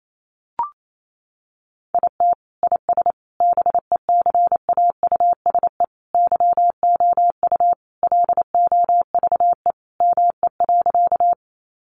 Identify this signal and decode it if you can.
Morse code, heard as a keyed tone, about 28 wpm, 700 Hz. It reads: IT IS BECAUSE YOU LOVE ME.
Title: Sentences from Top 600 Words 28wpm